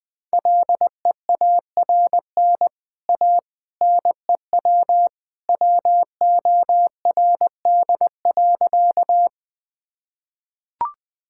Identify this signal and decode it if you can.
Morse code, heard as a keyed tone, about 20 wpm, 700 Hz. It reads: LEARN A NEW WORD.
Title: Easy Sentences with Hints - 20wpm